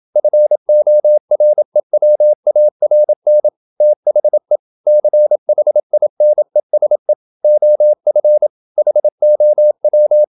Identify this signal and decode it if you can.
Morse code, heard as a keyed tone, about 27 wpm, 600 Hz. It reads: FOREWARN THE CHINESE OF HOW
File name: Never 0027wpm